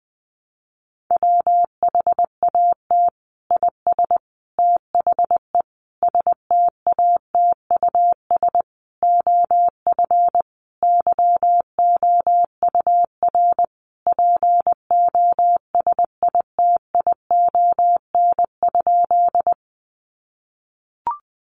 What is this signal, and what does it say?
Morse code, heard as a keyed tone, about 20 wpm, 700 Hz. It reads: WHAT IS THE STATUS OF YOUR POSITION?
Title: Sentences from Top 1400 Words 20wpm